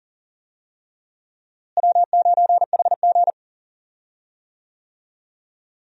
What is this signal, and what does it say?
Morse code, heard as a keyed tone, about 40 wpm, 700 Hz. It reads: W9HG
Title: USA Call Signs - 4 Characters - Repeated 3x - 40wpm